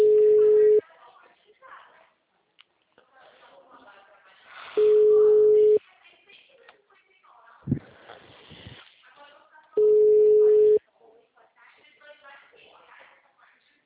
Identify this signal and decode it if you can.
Ringback tone (Europe, 425 Hz), with speech.
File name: out- -2011- - - .69479